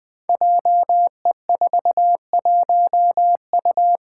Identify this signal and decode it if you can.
Morse code, heard as a keyed tone, about 20 wpm, 700 Hz. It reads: JE41U
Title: Five Challenging Letter-Number Combinations 20wpm